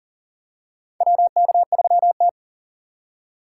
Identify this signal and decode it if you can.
Morse code, heard as a keyed tone, about 40 wpm, 700 Hz. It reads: WK3T
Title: USA Call Signs - 4 Characters - 4x Character Spacing - 40wpm